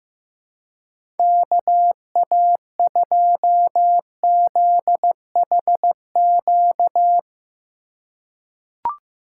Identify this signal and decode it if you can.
Morse code, heard as a keyed tone, about 16 wpm, 700 Hz. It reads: KA2ZHQ
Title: USA Call Signs - 6 Characters - 2x Character Spacing - 15wpm